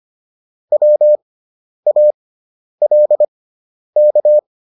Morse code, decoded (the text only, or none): WALK